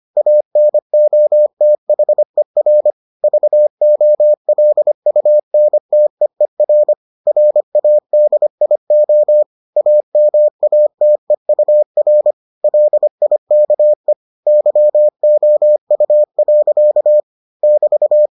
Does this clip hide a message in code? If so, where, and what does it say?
Yes: Morse code from 0.2 to 18.3 s, heard as a keyed tone, about 25 wpm, 600 Hz. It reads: ANOTHER VOLUNTEER RADIO AMATEUR LIKE YOU. <BT>